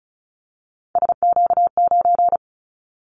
Morse code, heard as a keyed tone, about 36 wpm, 700 Hz. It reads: SQ9
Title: Callsign Prefixes for Top 25 DXCC - 35wpm